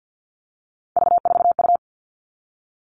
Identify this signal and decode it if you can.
Morse code, heard as a keyed tone, about 50 wpm, 700 Hz. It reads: V4U
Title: All Cut Numbers - 3-Digit 50wpm